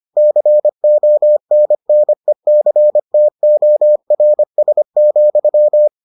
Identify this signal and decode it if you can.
Morse code, heard as a keyed tone, about 25 wpm, 600 Hz. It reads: CONNECTORS,